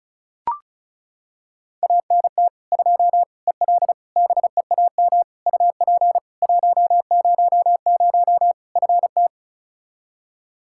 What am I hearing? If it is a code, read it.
Morse code, heard as a keyed tone, about 34 wpm, 700 Hz. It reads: ANT 2 EL BEAM UP 100 FT
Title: Practice QSO - Exchange 2 - Info - 35wpm